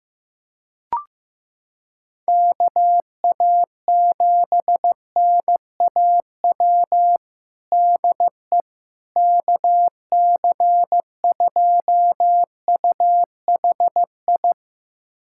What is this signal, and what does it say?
Morse code, heard as a keyed tone, about 15 wpm, 700 Hz. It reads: KA7NAW DE KC2UHI
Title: USA Callsign DE Callsign 15wpm